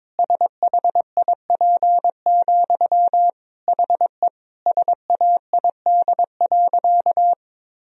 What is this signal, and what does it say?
Morse code, heard as a keyed tone, about 22 wpm, 700 Hz. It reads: SHIP, HE SAID.
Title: Sentences from Top 1000 Words - Review Part 2 - 22wpm